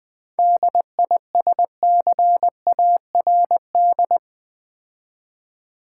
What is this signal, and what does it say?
Morse code, heard as a keyed tone, about 20 wpm, 700 Hz. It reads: DISCARD
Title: 7 Letter Words - First 3 Letters Given - 20wpm